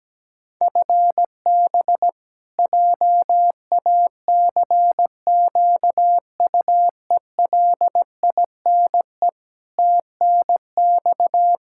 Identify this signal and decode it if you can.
Morse code, heard as a keyed tone, about 17 wpm, 700 Hz. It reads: FB JACQUELINE TNX
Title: Practice QSO - Exchange 3 - More Info - 17wpm